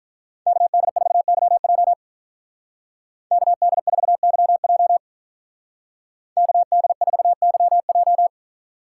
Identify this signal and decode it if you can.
Morse code, heard as a keyed tone, about 42 wpm, 700 Hz. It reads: KD4YJ KD4YJ KD4YJ
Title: Speed Racing - USA 2x2 Call Signs 35wpm